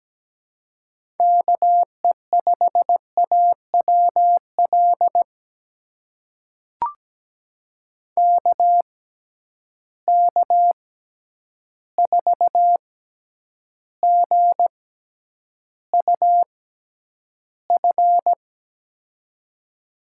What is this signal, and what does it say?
Morse code, heard as a keyed tone, about 17 wpm, spaced as for 9 wpm, 700 Hz. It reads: KE5AWL KK4GUF
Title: USA Call Signs - 6 Characters - 6x Character Spacing - 17wpm